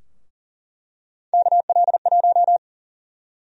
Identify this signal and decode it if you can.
Morse code, heard as a keyed tone, about 40 wpm, 700 Hz. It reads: KL1
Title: USA Call Sign Prefixes - Repeated 2x - 40wpm